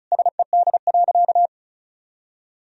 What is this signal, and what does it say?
Morse code, heard as a keyed tone, about 36 wpm, 700 Hz. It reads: SED.
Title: Sentences from Top 500 Words and Abbreviations 35wpm